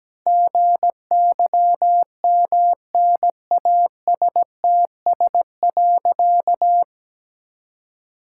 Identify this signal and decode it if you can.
Morse code, heard as a keyed tone, about 17 wpm, 700 Hz. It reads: GYMNASTS.
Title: Pangram Challenge v2 - 17wpm